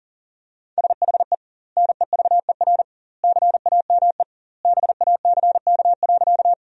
Morse code, heard as a keyed tone, about 40 wpm, 700 Hz. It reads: SHE NEVER CAME BACK.
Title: 2x Word-Spacing - Sentences from Top 300 Words - 40wpm